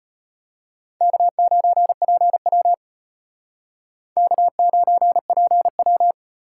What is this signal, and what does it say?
Morse code, heard as a keyed tone, about 32 wpm, 700 Hz. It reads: K9PW K9PW